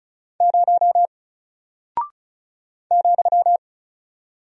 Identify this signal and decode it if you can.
Morse code, heard as a keyed tone, about 36 wpm, 700 Hz. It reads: 0 ,
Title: Single Character (Letter:Number:Punctuation:Prosign) Alt 35wpm